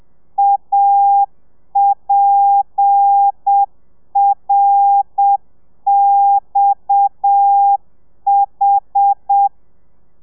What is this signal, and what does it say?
Morse code, heard as a keyed tone, about 7 wpm, 800 Hz. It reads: APRXH
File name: telegrafia / i2tfj / Lez7 30